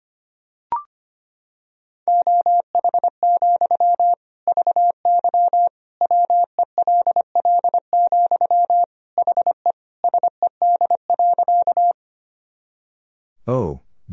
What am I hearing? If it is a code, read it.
Morse code, heard as a keyed tone, about 25 wpm, 700 Hz. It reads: OH, VY WELL, HE SED.
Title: Sentences from Top 500 Words and Abbreviations 25wpm